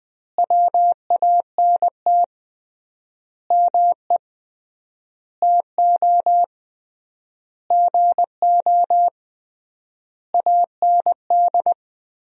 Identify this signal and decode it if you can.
Morse code, heard as a keyed tone, about 20 wpm, 700 Hz. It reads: WANT ME TO GO AND
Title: 3x Word-Spacing - Sentences from Top 200 Words - 20wpm